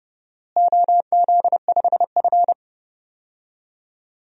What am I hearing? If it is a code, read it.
Morse code, heard as a keyed tone, about 30 wpm, 700 Hz. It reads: OZ5F
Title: DXCC-100 Call Signs - 4 Characters - Repeated 3x - 30wpm